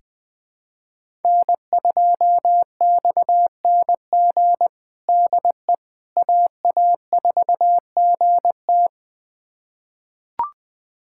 Morse code, heard as a keyed tone, about 20 wpm, 700 Hz. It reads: N2XNG DE AA4GT
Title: USA Callsign DE Callsign - 5 Characters - 20wpm